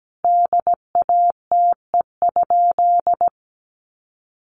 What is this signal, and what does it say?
Morse code, heard as a keyed tone, about 17 wpm, 700 Hz. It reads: DATE?
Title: Sentences from Top 1400 Words 17wpm